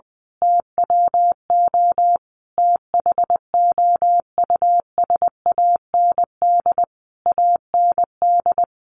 Morse code, heard as a keyed tone, about 20 wpm, 700 Hz. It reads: TWO THOUSAND AND
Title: A Sound Of Thunder 20wpm